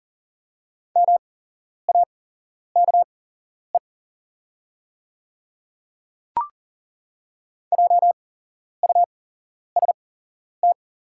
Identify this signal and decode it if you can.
Morse code, heard as a keyed tone, about 40 wpm, spaced as for 11 wpm, 700 Hz. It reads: MAKE JUST
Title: 8x Character Spacing - Top 200 Words - 40wpm